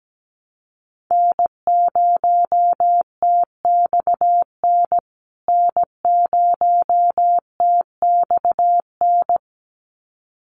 Morse code, heard as a keyed tone, about 17 wpm, 700 Hz. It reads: N0TXN N0TXN
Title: USA Call Signs Repeated 2x 17wpm